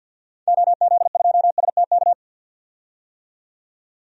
Morse code, heard as a keyed tone, about 50 wpm, 700 Hz. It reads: OZ2STK
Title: DXCC-100 Call Signs - 6 Characters - 50wpm